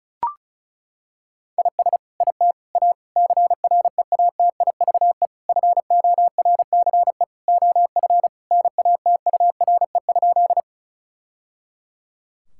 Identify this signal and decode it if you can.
Morse code, heard as a keyed tone, about 36 wpm, 700 Hz. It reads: IS IT A CREATIVE FORCE OF NATURE?
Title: Sentences from Top 1000 Words - Review Part 5 - 35wpm